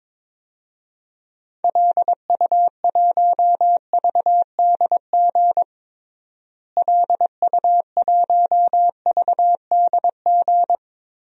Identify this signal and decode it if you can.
Morse code, heard as a keyed tone, about 22 wpm, 700 Hz. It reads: LU1VDG LU1VDG
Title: DXCC-100 Call Signs - 6 Characters - Repeated 2x - 22wpm